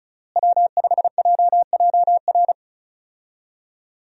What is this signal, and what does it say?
Morse code, heard as a keyed tone, about 34 wpm, 700 Hz. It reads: W5JJR
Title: USA Call Signs - 5 Characters - Repeated 3x - 35wpm